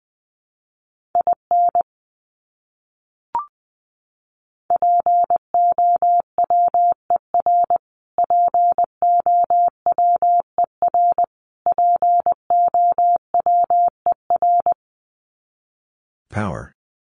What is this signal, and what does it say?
Morse code, heard as a keyed tone, about 20 wpm, 700 Hz. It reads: IN POWER POWER POWER
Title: Top 500 Words Repeated 20wpm